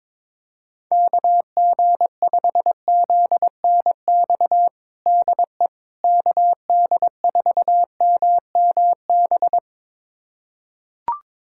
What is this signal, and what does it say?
Morse code, heard as a keyed tone, about 22 wpm, 700 Hz. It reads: KG5ZNX DE KD4MMB
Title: USA Callsign DE Callsign 22wpm